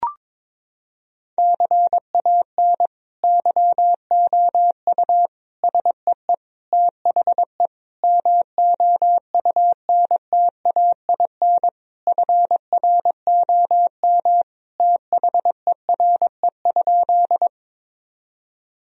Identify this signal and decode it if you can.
Morse code, heard as a keyed tone, about 22 wpm, 700 Hz. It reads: CAN YOU SEE THE MOUNTAIN FROM THERE?
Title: Sentences from Top 1000 Words - Review Part 1 - 22wpm